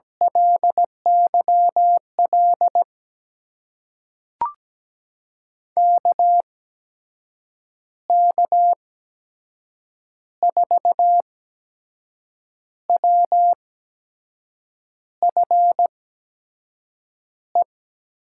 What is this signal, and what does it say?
Morse code, heard as a keyed tone, about 17 wpm, spaced as for 6 wpm, 700 Hz. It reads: LYL KK4WFE
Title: USA Call Signs - 8x Character Spacing - 17wpm